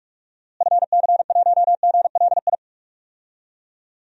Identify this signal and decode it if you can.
Morse code, heard as a keyed tone, about 41 wpm, 700 Hz. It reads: FC1GLI
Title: DXCC-100 Call Signs - 6 Characters - Repeated 2x - 45wpm